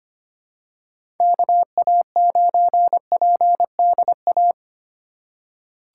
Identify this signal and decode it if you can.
Morse code, heard as a keyed tone, about 25 wpm, 700 Hz. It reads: KA9PDA
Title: USA Call Signs - 2x Character Spacing - 25wpm